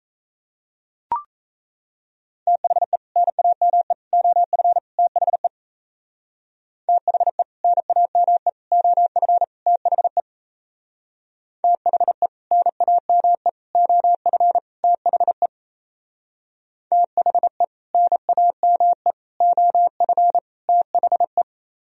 Morse code, heard as a keyed tone, about 34 wpm, 700 Hz. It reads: THE NAME OF THE THE NAME OF THE THE NAME OF THE THE NAME OF THE
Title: Speed-Racing - Sets of 4 Words 28wpm